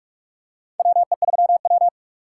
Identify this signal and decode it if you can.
Morse code, heard as a keyed tone, about 41 wpm, 700 Hz. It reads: WE3W
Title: USA Call Signs - 4 Characters - 3x Character Spacing - 45wpm